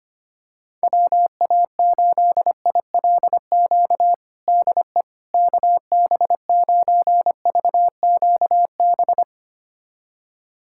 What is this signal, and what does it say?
Morse code, heard as a keyed tone, about 25 wpm, 700 Hz. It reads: WA8ILQ DE KB9VQB
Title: USA Callsign DE Callsign 25wpm